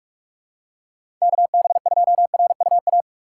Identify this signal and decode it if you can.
Morse code, heard as a keyed tone, about 46 wpm, 700 Hz. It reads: KB2RUA
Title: USA Call Signs - 6 Characters - 6x Character Spacing - 45wpm